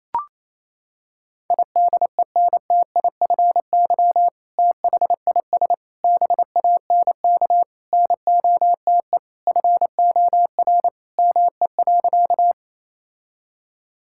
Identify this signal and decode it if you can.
Morse code, heard as a keyed tone, about 28 wpm, 700 Hz. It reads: IDENTIFY THIS BANK NOTE FOR ME.
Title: Sentences from Top 1000 Words - Review Part 6 - 28wpm